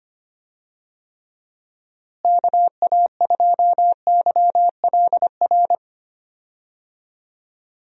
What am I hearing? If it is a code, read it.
Morse code, heard as a keyed tone, about 25 wpm, 700 Hz. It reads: KA2YLR
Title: USA Call Signs - 6 Characters - Repeated 3x - 25wpm